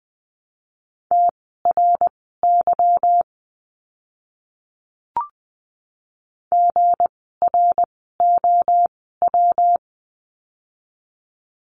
Morse code, heard as a keyed tone, about 19 wpm, spaced as for 14 wpm, 700 Hz. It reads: TRY GROW
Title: 160 Lesson 26f - Letter g - Words 20wpm